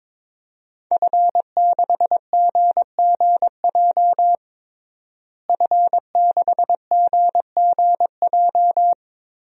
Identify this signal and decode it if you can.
Morse code, heard as a keyed tone, about 21 wpm, 700 Hz. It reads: F6GGJ F6GGJ
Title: DXCC-100 Call Signs - 5 Characters - Repeated 2x - 22wpm